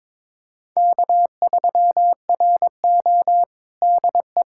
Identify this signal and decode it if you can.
Morse code, heard as a keyed tone, about 22 wpm, 700 Hz. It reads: K3RO DE
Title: USA Callsign DE Callsign - 4 Characters - 22wpm